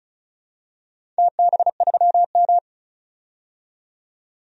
Morse code, heard as a keyed tone, about 36 wpm, 700 Hz. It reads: TB3M
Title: DXCC-100 Call Signs - 4 Characters - 35wpm